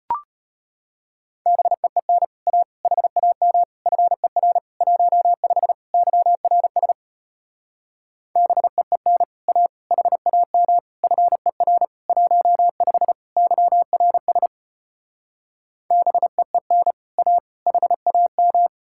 Morse code, heard as a keyed tone, about 33 wpm, 700 Hz. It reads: BEEN A HAM FER 15 YRS BEEN A HAM FER 15 YRS BEEN A HAM